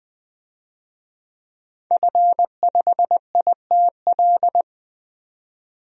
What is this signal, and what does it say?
Morse code, heard as a keyed tone, about 20 wpm, 700 Hz. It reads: F5ITL
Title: DXCC-100 Call Signs - 5 Characters - 20wpm